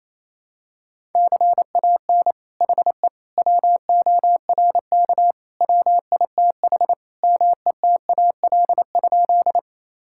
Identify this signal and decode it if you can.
Morse code, heard as a keyed tone, about 28 wpm, 700 Hz. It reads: CAN HE WORK WITH METAL?
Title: Sentences from Top 1000 Words - Review Part 6 - 28wpm